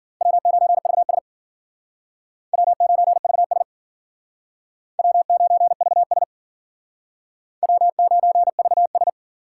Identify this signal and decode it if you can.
Morse code, heard as a keyed tone, about 51 wpm, 700 Hz. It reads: W9VS W9VS W9VS W9VS